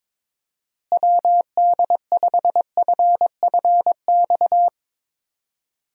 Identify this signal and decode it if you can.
Morse code, heard as a keyed tone, about 22 wpm, 700 Hz. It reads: WD5FFX
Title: USA Call Signs - 6 Characters - Repeated 3x - 22wpm